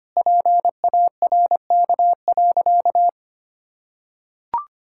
Morse code, heard as a keyed tone, about 25 wpm, 700 Hz. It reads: PARK.
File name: practice / 25wpm / Sentences with 2-Word Phrases - 25wpm